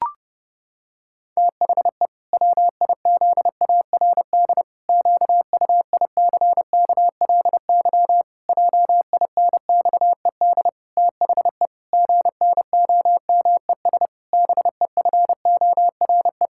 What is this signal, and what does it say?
Morse code, heard as a keyed tone, about 32 wpm, 700 Hz. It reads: THE WIZARD QUICKLY JINXED THE GNOMES BEFORE
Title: Pangram Challenge v2 - 30wpm